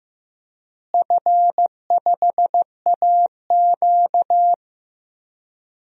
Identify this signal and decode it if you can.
Morse code, heard as a keyed tone, about 15 wpm, 700 Hz. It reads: F5AQ
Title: DXCC-100 Call Signs - 4 Characters - Repeated 3x - 15wpm